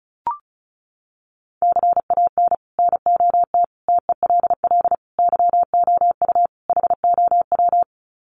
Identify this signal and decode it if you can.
Morse code, heard as a keyed tone, about 34 wpm, 700 Hz. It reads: CAN NOT TELL YOU HOW
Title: Sets of 5 Words Encore2 35wpm